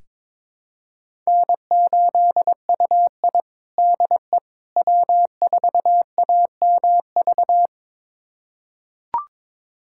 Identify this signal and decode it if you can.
Morse code, heard as a keyed tone, about 22 wpm, 700 Hz. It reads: N8UI DE W4AMV
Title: USA Callsign DE Callsign 22wpm